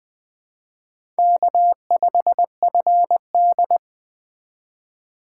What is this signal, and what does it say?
Morse code, heard as a keyed tone, about 21 wpm, 700 Hz. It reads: K5FD
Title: Speed Racing - USA 1x2 Call Signs 17wpm